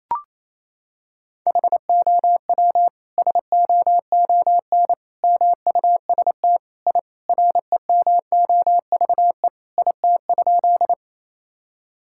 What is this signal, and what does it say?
Morse code, heard as a keyed tone, about 27 wpm, 700 Hz. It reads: HOW SOON MUST I REMOVE IT?
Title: Sentences from Top 1000 Words - Review Part 5 - 28wpm